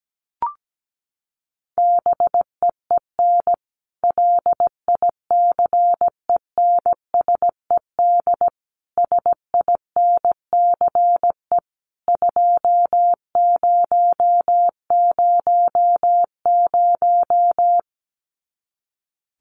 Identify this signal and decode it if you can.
Morse code, heard as a keyed tone, about 17 wpm, 700 Hz. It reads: BEEN LICENSED SINCE 2000
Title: Common QSO Phrases 17wpm